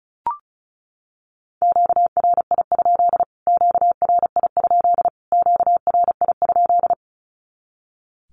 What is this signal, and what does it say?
Morse code, heard as a keyed tone, about 34 wpm, 700 Hz. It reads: QRI? QRI? QRI?
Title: Q-Codes Repeated 3x 35wpm